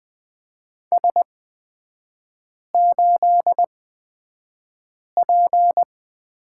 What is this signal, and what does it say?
Morse code, heard as a keyed tone, about 20 wpm, spaced as for 5 wpm, 700 Hz. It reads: S8P
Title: 255 Lesson 41e - Period - Three Characters 20wpm